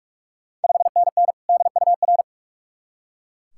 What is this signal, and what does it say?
Morse code, heard as a keyed tone, about 41 wpm, 700 Hz. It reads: 5NN DUR
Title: Ontario QSO Party Exchange 45wpm